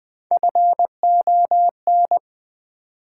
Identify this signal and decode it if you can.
Morse code, heard as a keyed tone, about 20 wpm, 700 Hz. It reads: FON
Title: Build A Word - 1 Letter At A Time - 4-Letter Words - 20wpm